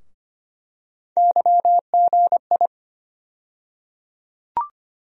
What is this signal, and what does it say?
Morse code, heard as a keyed tone, about 25 wpm, 700 Hz. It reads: YGI
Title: USA Call Sign Postfixes - 3 Letters 25wpm